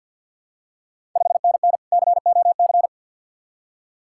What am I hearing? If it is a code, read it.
Morse code, heard as a keyed tone, about 51 wpm, 690 Hz. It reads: 5NN COC
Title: Ontario QSO Party Exchange 50wpm